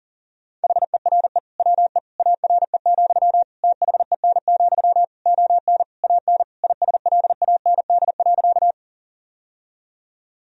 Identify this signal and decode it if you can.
Morse code, heard as a keyed tone, about 40 wpm, 700 Hz. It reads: HERE WE ARE, THEN, ON AN ISLAND.
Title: Sentences from Top 1000 Words - Review Part 3 - 40wpm